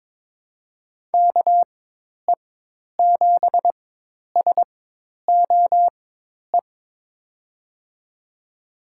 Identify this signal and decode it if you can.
Morse code, heard as a keyed tone, about 22 wpm, spaced as for 10 wpm, 700 Hz. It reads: KE7SOE
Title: USA Call Signs - 6 Characters - 4x Character Spacing - 22wpm